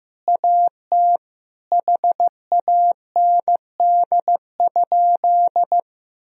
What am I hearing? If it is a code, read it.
Morse code, heard as a keyed tone, about 15 wpm, 700 Hz. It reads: AT HAND?
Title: Sentences from Top 1000 Words - Review Part 2 - 15wpm